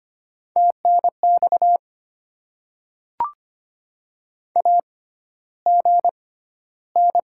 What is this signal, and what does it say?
Morse code, heard as a keyed tone, about 25 wpm, spaced as for 13 wpm, 700 Hz. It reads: TNX AGN
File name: Top 20 QSO Elements - 6x Character Spacing - 25wpm